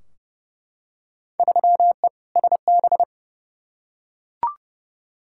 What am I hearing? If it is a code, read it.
Morse code, heard as a keyed tone, about 31 wpm, 700 Hz. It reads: 3E SB
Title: Field Day Exchanges Encore - 30wpm